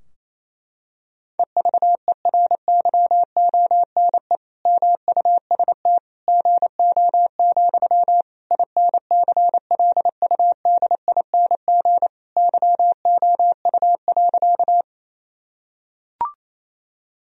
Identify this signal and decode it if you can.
Morse code, heard as a keyed tone, about 28 wpm, 700 Hz. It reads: EVERYONE MUST GO, INCLUDING YOU.
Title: Sentences from Top 1000 Words - Review Part 4 - 28wpm